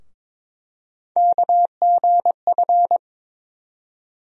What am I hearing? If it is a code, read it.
Morse code, heard as a keyed tone, about 21 wpm, 700 Hz. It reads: KGF
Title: USA Call Sign Postfixes - 3 Letters 22wpm